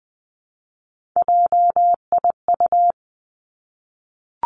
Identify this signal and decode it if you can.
Morse code, heard as a keyed tone, about 20 wpm, 700 Hz. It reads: JIU
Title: Three Challenging Letter-Number Combinations 20wpm